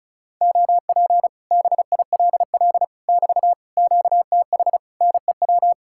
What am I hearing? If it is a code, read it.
Morse code, heard as a keyed tone, about 46 wpm, 700 Hz. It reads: OP BILL <BT> QTH NEW